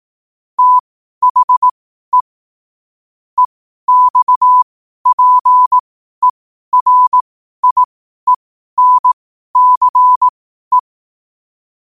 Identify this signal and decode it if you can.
Morse code, heard as a keyed tone, about 18 wpm, spaced as for 12 wpm, 1000 Hz. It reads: THE EXPERIENCE
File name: qotdfiles / 12wpm